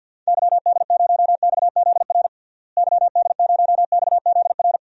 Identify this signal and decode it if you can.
Morse code, heard as a keyed tone, about 52 wpm, 700 Hz. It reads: YD0XZR YD0XZR
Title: DXCC-100 Call Signs - 6 Characters - Repeated 2x - 50wpm